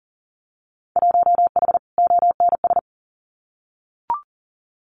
Morse code, heard as a keyed tone, about 40 wpm, 700 Hz. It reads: 1H ONS
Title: Winter Field Day Exchanges 40wpm